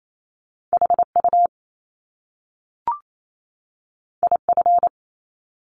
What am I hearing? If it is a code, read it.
Morse code, heard as a keyed tone, about 28 wpm, 700 Hz. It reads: HU IF
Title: Top Two Letter Combinations in Words 28wpm